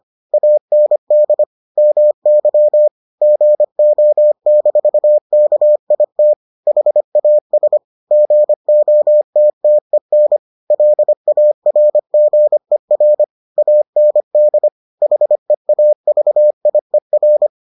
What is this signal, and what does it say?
Morse code, heard as a keyed tone, about 25 wpm, 600 Hz. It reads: AND MY GO-KIT HAS GOTTEN LARGER AND HEAVIER